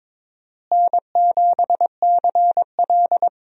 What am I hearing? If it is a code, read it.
Morse code, heard as a keyed tone, about 22 wpm, 700 Hz. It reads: N7CL
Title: USA Call Signs - 4 Characters - Repeated 3x - 22wpm